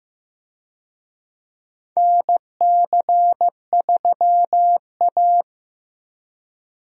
Morse code, heard as a keyed tone, about 15 wpm, 700 Hz. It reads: NC3A
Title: USA Call Signs - 4 Characters - 15wpm